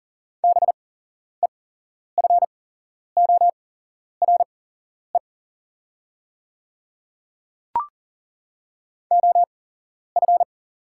Morse code, heard as a keyed tone, about 40 wpm, spaced as for 11 wpm, 700 Hz. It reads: BEFORE OF
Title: 8x Character Spacing - Top 300 Words - 40wpm